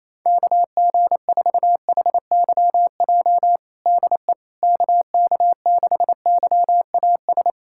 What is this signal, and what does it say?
Morse code, heard as a keyed tone, about 27 wpm, 700 Hz. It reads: KG4HYJ DE KK6YAS
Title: USA Callsign DE Callsign - 6 Characters - 28wpm